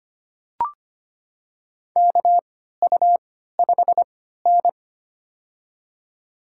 Morse code, heard as a keyed tone, about 25 wpm, spaced as for 14 wpm, 700 Hz. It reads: KU5N
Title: USA Call Signs - 4 Characters - 3x Character Spacing - 25wpm